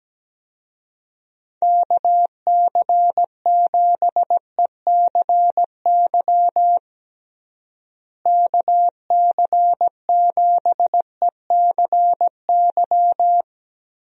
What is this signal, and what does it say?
Morse code, heard as a keyed tone, about 17 wpm, 700 Hz. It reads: KC7ECY KC7ECY
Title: USA Call Signs - 6 Characters - Repeated 3x - 17wpm